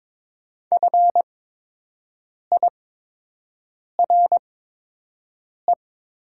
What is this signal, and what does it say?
Morse code, heard as a keyed tone, about 22 wpm, spaced as for 6 wpm, 700 Hz. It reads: FIRE
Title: 8x Character Spacing - Top 500 Words - 22wpm